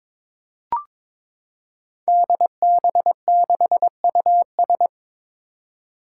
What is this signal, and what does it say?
Morse code, heard as a keyed tone, about 22 wpm, 700 Hz. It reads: DB6US
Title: Five Challenging Letter-Number Combinations 22wpm